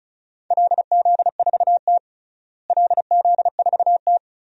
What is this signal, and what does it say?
Morse code, heard as a keyed tone, about 36 wpm, 700 Hz. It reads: LZ4T LZ4T
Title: DXCC-100 Call Signs - 4 Characters - Repeated 3x - 35wpm